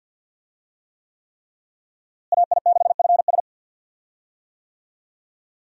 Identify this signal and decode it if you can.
Morse code, heard as a keyed tone, about 52 wpm, 700 Hz. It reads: AI6FS
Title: USA Call Signs - 5 Characters - Repeated 3x - 50wpm